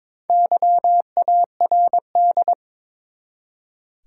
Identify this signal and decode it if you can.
Morse code, heard as a keyed tone, about 22 wpm, 700 Hz. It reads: YARD
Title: Four Letter Words - 22wpm